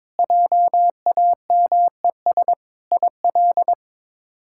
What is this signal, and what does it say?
Morse code, heard as a keyed tone, about 21 wpm, 700 Hz. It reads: JAMES IL
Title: K1USN SST QSO Practice - 22wpm